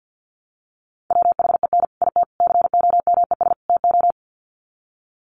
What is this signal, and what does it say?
Morse code, heard as a keyed tone, about 49 wpm, 700 Hz. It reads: WHEN IT COMES TO
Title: Sets of 4 Words 50wpm